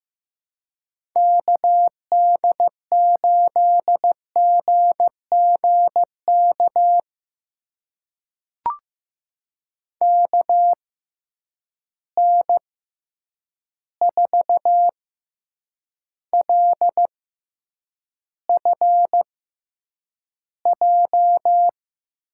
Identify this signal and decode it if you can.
Morse code, heard as a keyed tone, about 15 wpm, spaced as for 8 wpm, 700 Hz. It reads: KD8GGK KN4LFJ
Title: USA Call Signs - 6 Characters - 6x Character Spacing - 15wpm